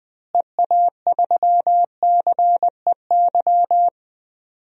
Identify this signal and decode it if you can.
Morse code, heard as a keyed tone, about 20 wpm, 700 Hz. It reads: EA3CEY
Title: DXCC-100 Call Signs - 6 Characters - Repeated 2x - 20wpm